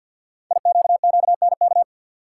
Morse code, heard as a keyed tone, about 52 wpm, 700 Hz. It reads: IQQNK
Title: Code Groups - 5 Letters - 50wpm